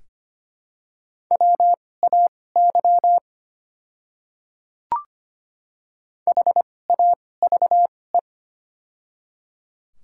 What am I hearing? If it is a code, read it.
Morse code, heard as a keyed tone, about 25 wpm, spaced as for 18 wpm, 700 Hz. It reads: WAY HAVE
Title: 2x Character Spacing - Top 200 Words - 25wpm